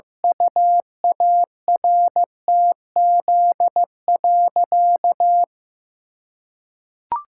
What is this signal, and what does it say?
Morse code, heard as a keyed tone, about 16 wpm, 700 Hz. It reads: UARTZ.
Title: Pangram Challenge v2 - 15wpm